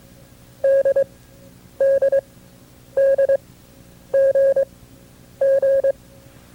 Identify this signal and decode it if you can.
Morse code, heard as a keyed tone, about 22 wpm, spaced as for 9 wpm, 550 Hz. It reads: DDDGG
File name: FreeFiles / CW-practice / KB6MT-tape2-side2